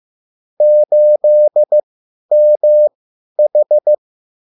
Morse code, heard as a keyed tone, about 15 wpm, spaced as for 10 wpm, 600 Hz. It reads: 8MH
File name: QSL / CW lesson